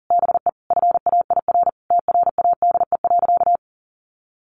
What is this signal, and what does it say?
Morse code, heard as a keyed tone, about 37 wpm, 700 Hz. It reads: BE FAIR TRADE.
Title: Sentences from Top 1300 Words 40wpm